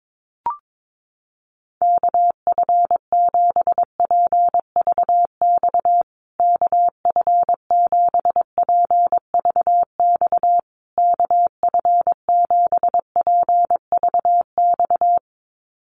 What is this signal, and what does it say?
Morse code, heard as a keyed tone, about 22 wpm, 700 Hz. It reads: KF7PVX KF7PVX KF7PVX
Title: USA Call Signs Repeated 3x 22wpm